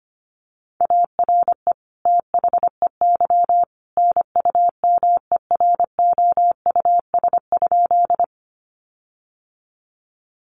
Morse code, heard as a keyed tone, about 25 wpm, 700 Hz. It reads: ARE THEY NUMEROUS?
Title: Sentences from Top 1400 Words 25wpm